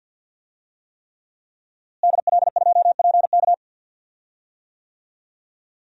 Morse code, heard as a keyed tone, about 46 wpm, 700 Hz. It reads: DL2PK
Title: DXCC-100 Call Signs - 5 Characters - Repeated 3x - 50wpm